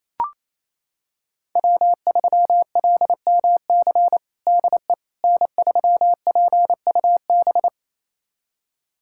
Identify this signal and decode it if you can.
Morse code, heard as a keyed tone, about 28 wpm, 700 Hz. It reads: W3LMC DE N3PUB
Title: USA Callsign DE Callsign - 5 Characters - 28wpm